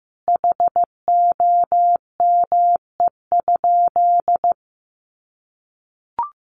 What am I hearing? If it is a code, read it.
Morse code, heard as a keyed tone, about 15 wpm, 700 Hz. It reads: HOME?
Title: Sentences from Top 1000 Words 15wpm